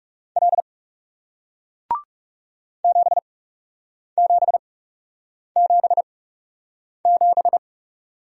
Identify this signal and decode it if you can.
Morse code, heard as a keyed tone, about 40 wpm, spaced as for 8 wpm, 700 Hz. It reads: L 7777